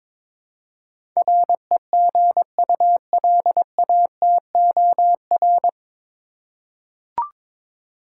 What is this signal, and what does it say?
Morse code, heard as a keyed tone, about 22 wpm, 700 Hz. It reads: REGULATOR
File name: 9 Letter Words - First 3 Letters Given - 22wpm